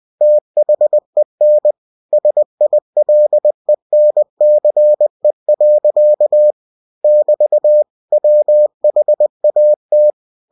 Morse code, heard as a keyed tone, about 20 wpm, 600 Hz. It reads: THEN SILENCE. <BT> WHAT
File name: cw-books / winnie-the-pooh / winnie-20wpm-page-0013